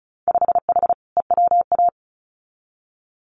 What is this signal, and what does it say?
Morse code, heard as a keyed tone, about 34 wpm, 700 Hz. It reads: 5H EWA
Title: Winter Field Day Exchanges Encore 35wpm